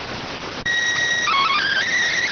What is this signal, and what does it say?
SSTV (slow-scan television), PD 290.